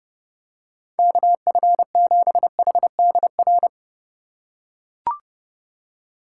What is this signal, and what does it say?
Morse code, heard as a keyed tone, about 30 wpm, 700 Hz. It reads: KF7HDR
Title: USA Call Signs - 6 Characters - 4x Character Spacing - 30wpm